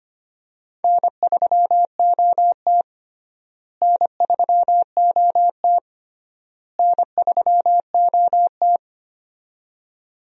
Morse code, heard as a keyed tone, about 25 wpm, 700 Hz. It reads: N3OT N3OT N3OT
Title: USA Call Signs - 4 Characters - Repeated 3x - 25wpm